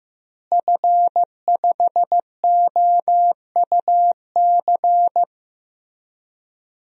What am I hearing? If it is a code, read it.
Morse code, heard as a keyed tone, about 15 wpm, 700 Hz. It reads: F5OUC
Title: DXCC-100 Call Signs - 5 Characters - Repeated 2x - 15wpm